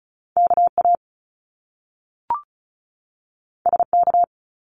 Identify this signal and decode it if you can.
Morse code, heard as a keyed tone, about 36 wpm, 700 Hz. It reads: KA SK
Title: Top Two Letter Combinations in Words 35wpm